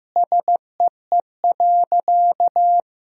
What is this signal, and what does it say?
Morse code, heard as a keyed tone, about 16 wpm, 700 Hz. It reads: SEE.
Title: Sentences from Top 1000 Words - Review Part 1 - 15wpm